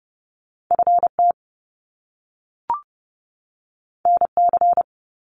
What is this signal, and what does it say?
Morse code, heard as a keyed tone, about 30 wpm, 700 Hz. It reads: FT NC
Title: Top Two Letter Combinations in Words 30wpm